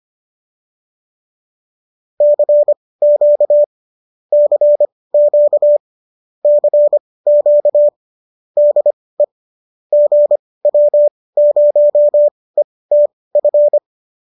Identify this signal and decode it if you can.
Morse code, heard as a keyed tone, about 25 wpm, spaced as for 18 wpm, 590 Hz. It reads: CQ CQ CQ DE GW0ETF
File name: qso218 18